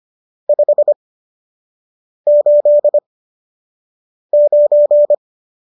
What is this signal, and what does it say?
Morse code, heard as a keyed tone, about 25 wpm, spaced as for 6 wpm, 590 Hz. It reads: 589
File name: qso003 06